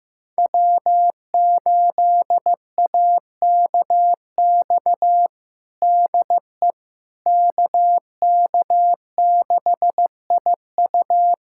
Morse code, heard as a keyed tone, about 15 wpm, 700 Hz. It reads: W8AKX DE KK6IU
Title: USA Callsign DE Callsign - 5 Characters - 15wpm